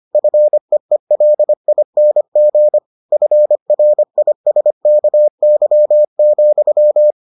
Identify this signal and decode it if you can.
Morse code, heard as a keyed tone, about 25 wpm, 600 Hz. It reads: FEELING FRISKY,